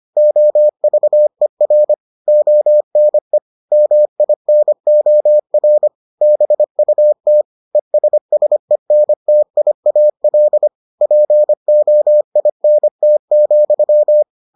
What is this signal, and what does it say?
Morse code, heard as a keyed tone, about 25 wpm, 600 Hz. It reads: OVER ONE MINOR BUT ESSENTIAL POINT,